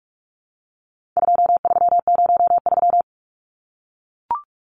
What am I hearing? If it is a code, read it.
Morse code, heard as a keyed tone, about 44 wpm, 700 Hz. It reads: 2303
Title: Numbers - 4-Digit 45wpm